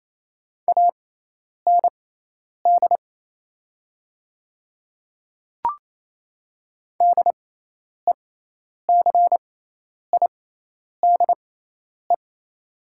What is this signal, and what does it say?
Morse code, heard as a keyed tone, about 27 wpm, spaced as for 10 wpm, 700 Hz. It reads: AND DECIDE
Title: 6x Character Spacing - Top 500 Words - 28wpm